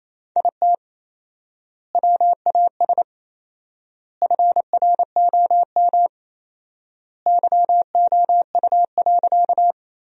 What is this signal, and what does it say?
Morse code, heard as a keyed tone, about 28 wpm, 700 Hz. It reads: IT WAS FROM YOU.
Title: 4x Word-Spacing - Sentences from Top 100 Words - 28wpm